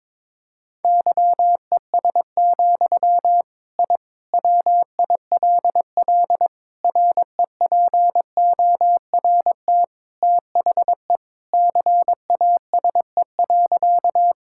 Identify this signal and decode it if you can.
Morse code, heard as a keyed tone, about 22 wpm, 700 Hz. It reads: YES, I WILL REPORT THE CASE.
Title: Sentences from Top 1000 Words - Review Part 6 - 22wpm